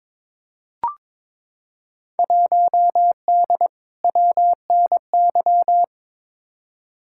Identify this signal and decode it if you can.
Morse code, heard as a keyed tone, about 22 wpm, 700 Hz. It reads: 1D WNY
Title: Field Day Exchanges - 22wpm